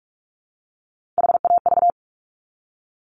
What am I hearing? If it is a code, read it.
Morse code, heard as a keyed tone, about 46 wpm, 700 Hz. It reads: HAV
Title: Top Three Letter Combinations in Words 45wpm